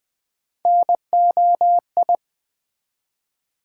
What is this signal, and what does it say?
Morse code, heard as a keyed tone, about 20 wpm, 700 Hz. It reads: NOI
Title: Build A Word - 1 Letter At A Time - 5-Letter Words - 20wpm